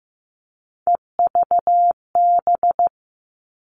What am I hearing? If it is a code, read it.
Morse code, heard as a keyed tone, about 15 wpm, 700 Hz. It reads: EVB
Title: All Cut Numbers - 3-Digit 15wpm